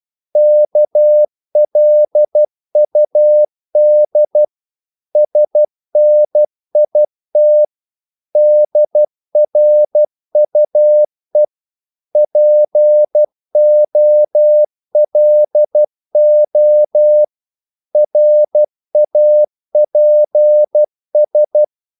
Morse code, heard as a keyed tone, about 12 wpm, 600 Hz. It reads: KLUD SNIT DRUE POLO RAPS
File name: Korte ord DK 12wpm